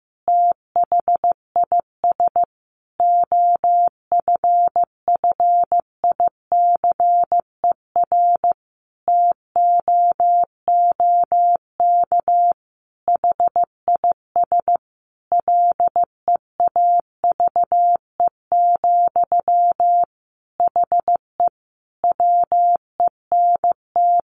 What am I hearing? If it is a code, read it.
Morse code, heard as a keyed tone, about 15 wpm, 700 Hz. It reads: THIS OFFICER TOOK HIS LEAVE, HE WENT
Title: Sentences from Top 1200 Words 15wpm